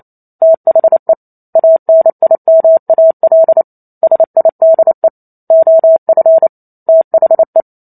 Morse code, heard as a keyed tone, about 28 wpm, 650 Hz. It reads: THE ANIMAL SIDE OF THE